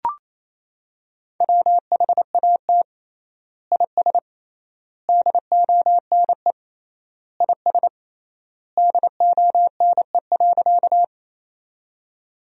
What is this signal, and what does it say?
Morse code, heard as a keyed tone, about 28 wpm, 700 Hz. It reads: WHAT IS DONE IS DONE.
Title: 3x Word-Spacing - Sentences from Top 500 Words - 28wpm